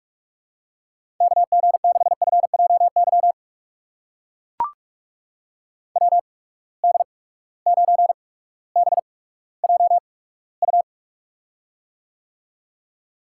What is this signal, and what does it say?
Morse code, heard as a keyed tone, about 44 wpm, spaced as for 19 wpm, 700 Hz. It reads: KG6FJY WD9BJU
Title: USA Call Signs - 6 Characters - 8x Character Spacing - 45wpm